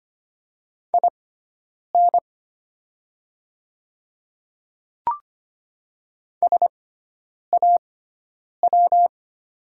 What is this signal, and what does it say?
Morse code, heard as a keyed tone, about 25 wpm, spaced as for 9 wpm, 700 Hz. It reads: IN SAW
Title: 6x Character Spacing - Top 300 Words - 25wpm